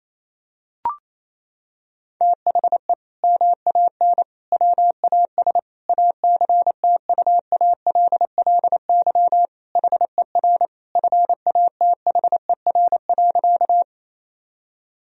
Morse code, heard as a keyed tone, about 27 wpm, 700 Hz. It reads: THE MAN WAS ACTUALLY HER FATHER.
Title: Sentences from Top 1000 Words - Review Part 4 - 28wpm